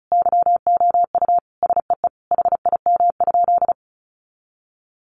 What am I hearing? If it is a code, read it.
Morse code, heard as a keyed tone, about 34 wpm, 700 Hz. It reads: YOU SEE HIM?
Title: Sentences from Top 100 Words 35wpm